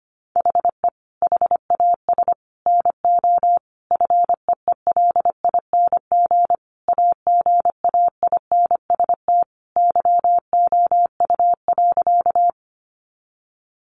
Morse code, heard as a keyed tone, about 25 wpm, 700 Hz. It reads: HE HAS NO FEELING AGAINST YOU.
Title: Sentences from Top 1200 Words 25wpm